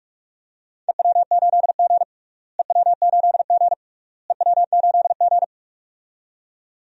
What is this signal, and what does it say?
Morse code, heard as a keyed tone, about 46 wpm, 700 Hz. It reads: EW8G EW8G EW8G
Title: DXCC-100 Call Signs - 4 Characters - Repeated 3x - 45wpm